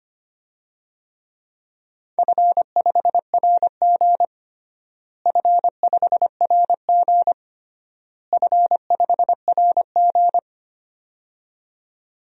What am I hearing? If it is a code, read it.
Morse code, heard as a keyed tone, about 25 wpm, 700 Hz. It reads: F5RG F5RG F5RG
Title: DXCC-100 Call Signs - 4 Characters - Repeated 3x - 25wpm